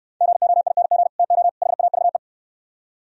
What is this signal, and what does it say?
Morse code, heard as a keyed tone, about 67 wpm, 700 Hz. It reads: GOING TO HAVE